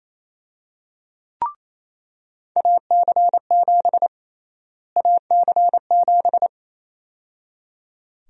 Morse code, heard as a keyed tone, about 27 wpm, 700 Hz. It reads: AC7 AC7
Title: USA Call Sign Prefixes - Repeated 2x - 28wpm